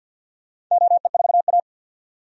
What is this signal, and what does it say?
Morse code, heard as a keyed tone, about 46 wpm, 700 Hz. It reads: OE4A
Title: DXCC-100 Call Signs - 4 Characters - 50wpm